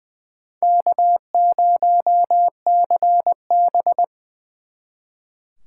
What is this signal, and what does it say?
Morse code, heard as a keyed tone, about 24 wpm, 700 Hz. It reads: K0CB
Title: Speed Racing - USA 1x2 Call Signs 20wpm